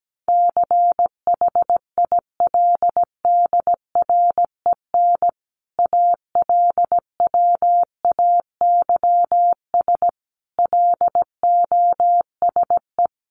Morse code, heard as a keyed tone, about 17 wpm, 700 Hz. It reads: CHILDREN ALWAYS LOSE
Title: Sentences from Top 1300 Words 17wpm